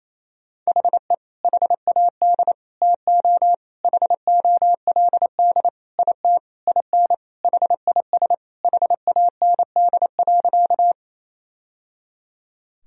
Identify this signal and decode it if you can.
Morse code, heard as a keyed tone, about 28 wpm, 700 Hz. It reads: HE HAD TO HOLD IT IN HIS HAND.
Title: Sentences from Top 1000 Words - Review Part 2 - 28wpm